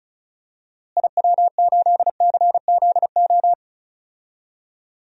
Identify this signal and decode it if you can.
Morse code, heard as a keyed tone, about 36 wpm, 700 Hz. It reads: IW8CZO
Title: DXCC-100 Call Signs - 6 Characters - Repeated 2x - 35wpm